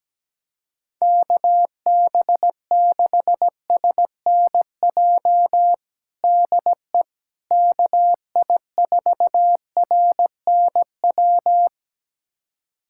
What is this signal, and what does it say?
Morse code, heard as a keyed tone, about 17 wpm, 700 Hz. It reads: KB6SNJ DE KI4RNW
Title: USA Callsign DE Callsign - 6 Characters - 17wpm